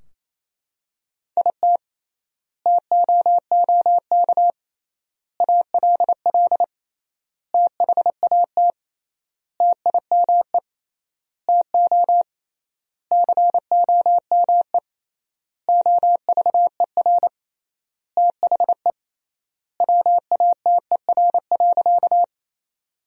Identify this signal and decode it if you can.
Morse code, heard as a keyed tone, about 28 wpm, 700 Hz. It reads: IT TOOK ALL THAT TIME TO COME OVER THE WATER.
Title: 3x Word-Spacing - Sentences from Top 300 Words - 28wpm